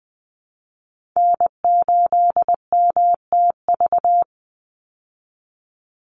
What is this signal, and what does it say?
Morse code, heard as a keyed tone, about 20 wpm, 700 Hz. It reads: N8MTV
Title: Call Signs Repeated 3x - 20wpm